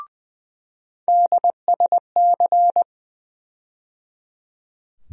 Morse code, heard as a keyed tone, about 20 wpm, 700 Hz. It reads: DSC
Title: USA Call Sign Postfixes - 3 Letters 20wpm